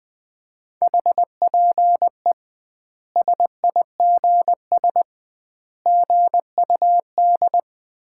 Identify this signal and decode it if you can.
Morse code, heard as a keyed tone, about 20 wpm, 700 Hz. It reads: HPE SIGS GUD
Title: Common QSO Phrases - 2x Word-Spacing - 20wpm